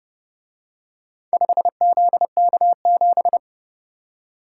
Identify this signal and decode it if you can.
Morse code, heard as a keyed tone, about 29 wpm, 700 Hz. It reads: 5ZK7